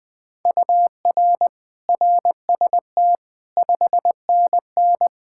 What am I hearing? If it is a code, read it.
Morse code, heard as a keyed tone, about 21 wpm, 700 Hz. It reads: UR RST 5NN
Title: Practice QSO - Exchange 1 - Beginning - 20wpm